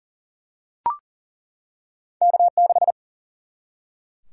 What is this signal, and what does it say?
Morse code, heard as a keyed tone, about 39 wpm, 700 Hz. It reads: K6
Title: Single-Letter Single-Number 40wpm